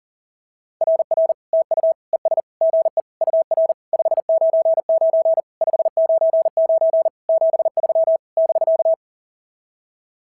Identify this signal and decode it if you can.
Morse code, heard as a keyed tone, about 44 wpm, 690 Hz. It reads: RR TU ES GE UR 599 599 73 <BK>
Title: SOTA QSO Practice 40wpm